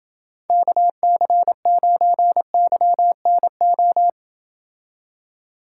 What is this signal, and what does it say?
Morse code, heard as a keyed tone, about 28 wpm, 700 Hz. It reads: KC9YNO